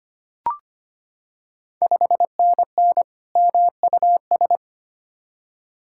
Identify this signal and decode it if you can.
Morse code, heard as a keyed tone, about 25 wpm, 700 Hz. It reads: 5NN MUS
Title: Ontario QSO Party Exchange 25wpm